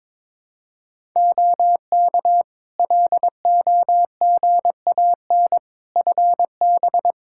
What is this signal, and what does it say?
Morse code, heard as a keyed tone, about 21 wpm, 700 Hz. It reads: OK LOGAN FB
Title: Practice QSO - Exchange 2 - Info - 22wpm